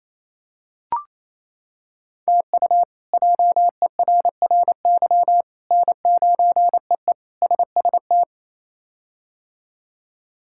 Morse code, heard as a keyed tone, about 28 wpm, 700 Hz. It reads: TU JERRY N9EE SST
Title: K1USN SST QSO Practice - 28wpm